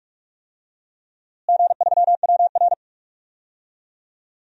Morse code, heard as a keyed tone, about 45 wpm, 700 Hz. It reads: G3WR
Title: DXCC-100 Call Signs - 4 Characters - Repeated 3x - 45wpm